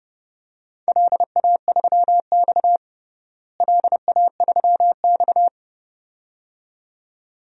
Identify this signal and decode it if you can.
Morse code, heard as a keyed tone, about 30 wpm, 700 Hz. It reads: LA3X LA3X
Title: DXCC-100 Call Signs - 4 Characters - Repeated 2x - 30wpm